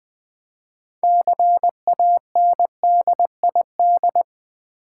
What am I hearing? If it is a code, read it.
Morse code, heard as a keyed tone, about 20 wpm, 700 Hz. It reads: CANDID
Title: Build A Word - 3 Letters At A Time - 9-Letter Words - 20wpm